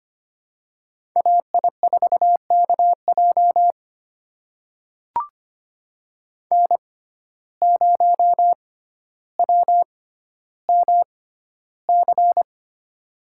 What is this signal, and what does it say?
Morse code, heard as a keyed tone, about 25 wpm, spaced as for 13 wpm, 690 Hz. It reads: AI4KJ N0WMC
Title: USA Call Signs - 5 Characters - 6x Character Spacing - 25wpm